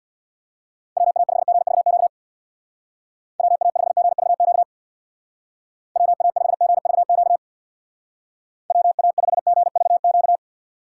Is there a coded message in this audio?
Yes: Morse code from 1.0 to 10.4 s, heard as a keyed tone, about 64 wpm, 700 Hz. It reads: WA5GVQ WA5GVQ WA5GVQ WA5GVQ